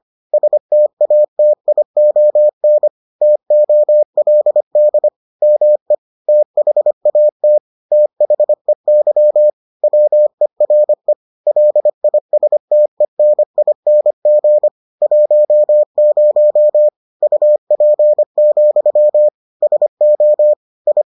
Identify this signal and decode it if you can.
Morse code, heard as a keyed tone, about 25 wpm, 600 Hz. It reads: STATION TOLD ME THAT THEY WERE LISTENING 10 UP, SO I